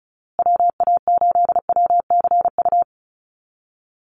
Morse code, heard as a keyed tone, about 34 wpm, 700 Hz. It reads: WA8WCU
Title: USA Call Signs Repeated 3x 35wpm